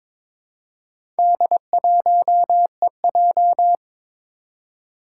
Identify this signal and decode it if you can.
Morse code, heard as a keyed tone, about 22 wpm, 700 Hz. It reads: D1EJ
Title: Four Challenging Letter-Number Combinations 22wpm